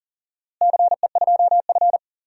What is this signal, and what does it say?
Morse code, heard as a keyed tone, about 40 wpm, 700 Hz. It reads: CE2F
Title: DXCC-100 Call Signs - 4 Characters - Repeated 3x - 40wpm